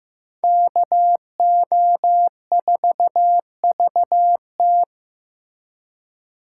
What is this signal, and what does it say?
Morse code, heard as a keyed tone, about 18 wpm, 700 Hz. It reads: KO4VT
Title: Speed Racing - USA 2x2 Call Signs 15wpm